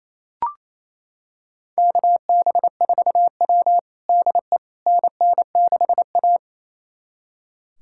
Morse code, heard as a keyed tone, about 28 wpm, 700 Hz. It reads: KB4W DE NN6A
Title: USA Callsign DE Callsign - 4 Characters - 28wpm